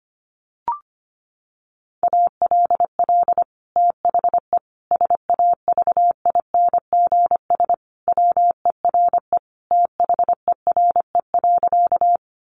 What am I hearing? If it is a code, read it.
Morse code, heard as a keyed tone, about 25 wpm, 700 Hz. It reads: ALL THE SAVINGS WERE THERE.
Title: Sentences from Top 1200 Words 25wpm